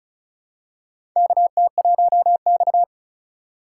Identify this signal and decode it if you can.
Morse code, heard as a keyed tone, about 36 wpm, 700 Hz. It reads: KT1X
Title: USA Call Signs - 4 Characters - 6x Character Spacing - 35wpm